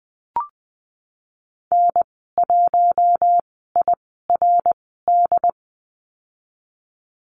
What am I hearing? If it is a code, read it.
Morse code, heard as a keyed tone, about 20 wpm, spaced as for 15 wpm, 700 Hz. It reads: N1IRD
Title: 128 Lesson 21g - Number 6 - Call Signs 20wpm